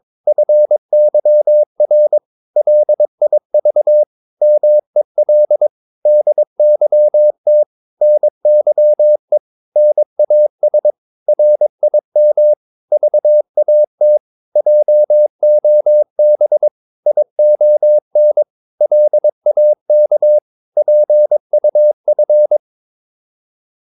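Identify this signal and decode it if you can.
Morse code, heard as a keyed tone, about 22 wpm, 600 Hz. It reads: FYR LIV MEL DYT NYE NAS RIM VAT JOB ION LAK PUF
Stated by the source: Morse practice of short Danish words at 22 wpm, from CW med Gnister.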